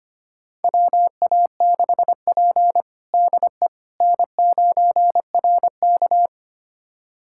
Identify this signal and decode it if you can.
Morse code, heard as a keyed tone, about 25 wpm, 700 Hz. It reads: WA6P DE N9RK
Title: USA Callsign DE Callsign - 4 Characters - 25wpm